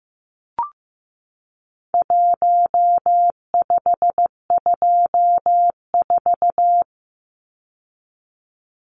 Morse code, heard as a keyed tone, about 15 wpm, 700 Hz. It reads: 1524
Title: Numbers - 4-Digit 15wpm